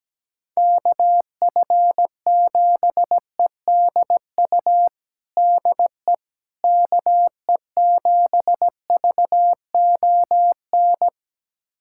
Morse code, heard as a keyed tone, about 17 wpm, 700 Hz. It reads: KF7EDU DE KE7VON
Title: USA Callsign DE Callsign - 6 Characters - 17wpm